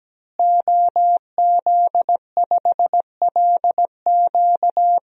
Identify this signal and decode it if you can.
Morse code, heard as a keyed tone, about 17 wpm, 700 Hz. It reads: OZ5LQ
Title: DXCC-100 Call Signs - 5 Characters - Repeated 3x - 17wpm